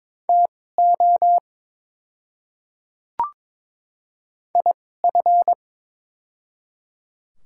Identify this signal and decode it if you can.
Morse code, heard as a keyed tone, about 22 wpm, spaced as for 16 wpm, 700 Hz. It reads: TO IF
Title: 2x Character Spacing - Top 100 Words - 22wpm